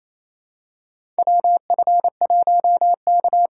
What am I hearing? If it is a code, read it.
Morse code, heard as a keyed tone, about 28 wpm, 700 Hz. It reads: WF1K
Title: USA Call Signs - 4x Character Spacing - 28wpm